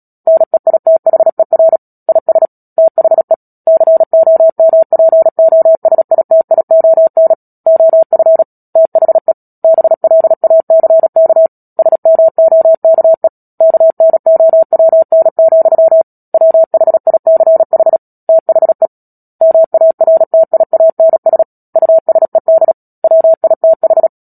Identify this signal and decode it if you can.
Morse code, heard as a keyed tone, about 37 wpm, 650 Hz. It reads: NEITHER IS THE COMPOSITION OF THE BLACK SMOKE KNOWN, WHICH THE MARTIANS USED WITH
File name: Book Two / 10-Chapter 10 - Epilogue - 36 WPM